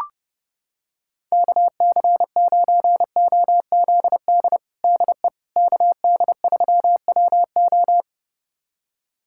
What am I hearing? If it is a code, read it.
Morse code, heard as a keyed tone, about 32 wpm, 700 Hz. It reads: KC9OZD DE KD3WO
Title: USA Callsign DE Callsign 30wpm